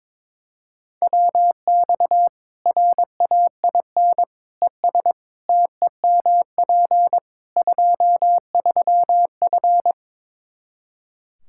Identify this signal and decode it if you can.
Morse code, heard as a keyed tone, about 22 wpm, 700 Hz. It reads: WX RAIN ES TEMP 23F
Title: K1USN SST QSO Practice 22wpm